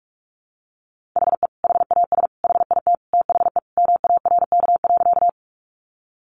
Morse code, heard as a keyed tone, about 44 wpm, 700 Hz. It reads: HE HAS HIT THE MARK.
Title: Sentences from Top 1200 Words 45wpm